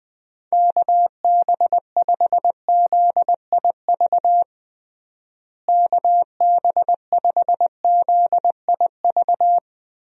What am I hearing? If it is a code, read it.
Morse code, heard as a keyed tone, about 20 wpm, 700 Hz. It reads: KB5ZIV KB5ZIV
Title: USA Call Signs - 6 Characters - Repeated 3x - 20wpm